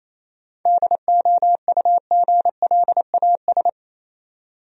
Morse code, heard as a keyed tone, about 28 wpm, 700 Hz. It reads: DOUGLAS
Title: USA Common Names - 28wpm